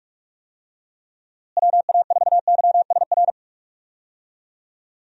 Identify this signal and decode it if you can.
Morse code, heard as a keyed tone, about 43 wpm, 700 Hz. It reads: WA4YSR
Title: USA Call Signs - 6 Characters - Repeated 3x - 45wpm